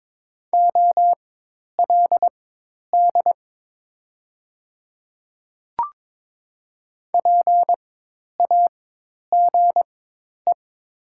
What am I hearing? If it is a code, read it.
Morse code, heard as a keyed tone, about 23 wpm, spaced as for 10 wpm, 700 Hz. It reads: OLD PAGE
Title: 4x Character Spacing - Top 300 Words - 22wpm